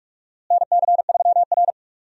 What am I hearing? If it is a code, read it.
Morse code, heard as a keyed tone, about 46 wpm, 700 Hz. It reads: NC3R
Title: USA Call Signs - 4 Characters - Repeated 2x - 45wpm